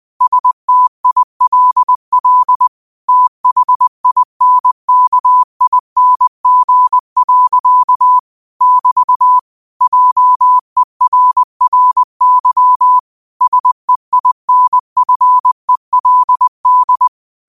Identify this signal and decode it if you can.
Morse code, heard as a keyed tone, about 20 wpm, 1000 Hz. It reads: STILL THINKING. <BT> JERRY SEINFELD
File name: qotdfiles / 20wpm